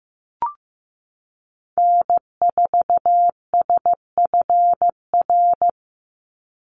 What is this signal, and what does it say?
Morse code, heard as a keyed tone, about 15 wpm, 700 Hz. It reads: N4SFR
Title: NA SOTA Activator Callsigns 15wpm